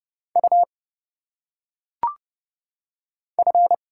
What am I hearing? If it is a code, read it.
Morse code, heard as a keyed tone, about 30 wpm, 700 Hz. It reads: U F
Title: Single Letters - Phonetic Pronunciation 30wpm